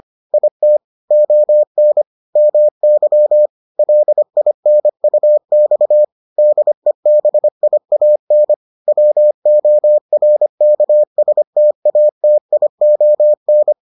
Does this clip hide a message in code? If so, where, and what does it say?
Yes: Morse code from 0.3 to 13.7 s, heard as a keyed tone, about 25 wpm, 600 Hz. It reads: IT ON MY LINUX DEBIAN WORKSTATION